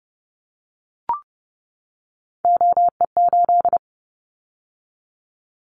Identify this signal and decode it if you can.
Morse code, heard as a keyed tone, about 30 wpm, 700 Hz. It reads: OE8
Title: Callsign Prefixes for Top 100 DXCC - 30wpm